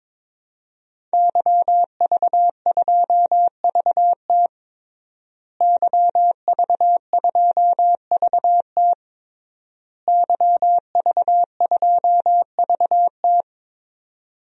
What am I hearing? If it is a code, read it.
Morse code, heard as a keyed tone, about 22 wpm, 700 Hz. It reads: YV2VT YV2VT YV2VT
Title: DXCC-100 Call Signs - 5 Characters - Repeated 3x - 22wpm